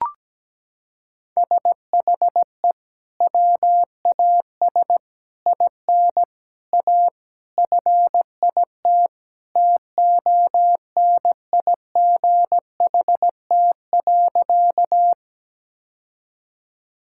Morse code, heard as a keyed tone, about 17 wpm, 700 Hz. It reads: SHE WAS IN A FIT TONIGHT.
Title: Sentences from Top 1000 Words - Review Part 5 - 17wpm